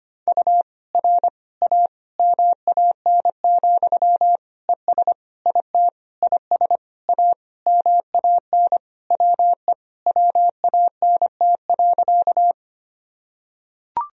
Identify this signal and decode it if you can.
Morse code, heard as a keyed tone, about 25 wpm, 700 Hz. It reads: U R A MAN, ES IT IS A MAN WE WANT.
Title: Sentences from Top 200 Words and Abbreviations 25wpm